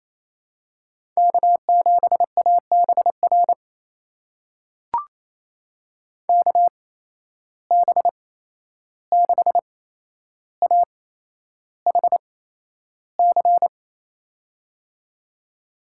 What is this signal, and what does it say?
Morse code, heard as a keyed tone, about 28 wpm, spaced as for 11 wpm, 700 Hz. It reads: K7ABR KB6AHC
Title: USA Call Signs - 8x Character Spacing - 28wpm